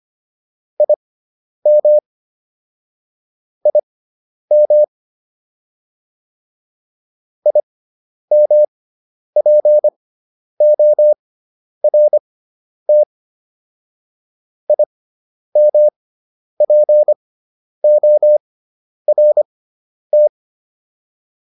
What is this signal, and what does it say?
Morse code, heard as a keyed tone, about 25 wpm, spaced as for 10 wpm, 600 Hz. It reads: IM IM IMPORT IMPORT
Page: intermediate-practice-files